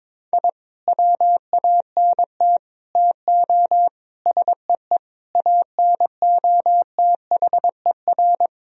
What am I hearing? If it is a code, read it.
Morse code, heard as a keyed tone, about 22 wpm, 700 Hz. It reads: I WANT TO SEE ANOTHER
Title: Sentences from Top 1000 Words - Review Part 1 - 22wpm